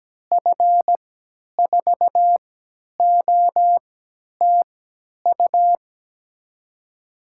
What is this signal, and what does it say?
Morse code, heard as a keyed tone, about 17 wpm, spaced as for 10 wpm, 700 Hz. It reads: F4OTU
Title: USA Call Signs - 6 Characters - 3x Character Spacing - 17wpm